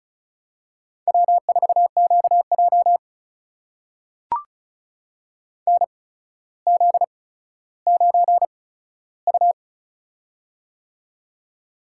Morse code, heard as a keyed tone, about 36 wpm, spaced as for 15 wpm, 700 Hz. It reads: W4QJ NZ9U
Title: USA Call Signs - 8x Character Spacing - 35wpm